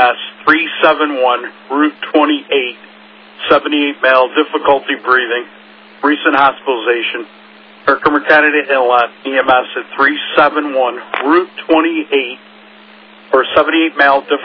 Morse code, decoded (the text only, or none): none